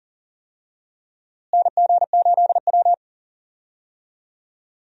Morse code, heard as a keyed tone, about 40 wpm, 700 Hz. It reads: NG8W
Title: USA Call Signs - 4 Characters - Repeated 2x - 40wpm